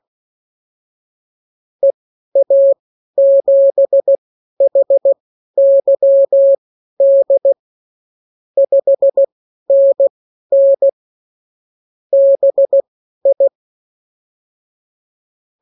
Morse code, heard as a keyed tone, about 16 wpm, spaced as for 12 wpm, 550 Hz. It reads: EA7HYD 5NN BI
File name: CWconcurso QSO libroCW4 16-8ppm